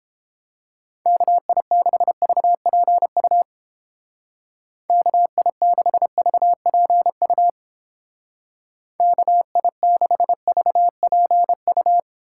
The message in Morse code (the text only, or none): KI6VPU KI6VPU KI6VPU